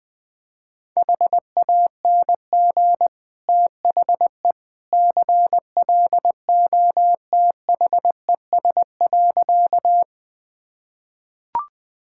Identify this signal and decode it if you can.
Morse code, heard as a keyed tone, about 21 wpm, 700 Hz. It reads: HANG THE CLOTHES.
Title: Easy Sentences with Hints - 20wpm